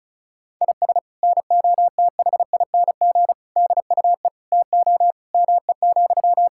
Morse code, heard as a keyed tone, about 36 wpm, 700 Hz. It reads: IS NOTHING DUE TO ME,
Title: Sentences from Top 1000 Words - Review Part 6 - 35wpm